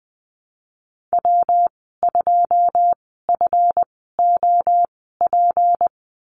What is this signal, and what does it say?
Morse code, heard as a keyed tone, about 20 wpm, spaced as for 15 wpm, 700 Hz. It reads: W2FOP
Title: 167 Lesson 27g - Number 7 - Call Signs 20wpm